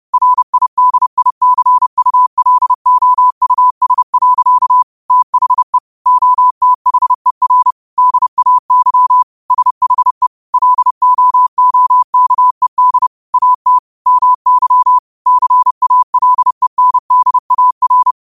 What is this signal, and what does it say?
Morse code, heard as a keyed tone, about 30 wpm, 1000 Hz. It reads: RIDICULOUS. THE OTHER DAY SHE LOOKED AT MY CALENDAR
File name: qotdfiles / 30wpm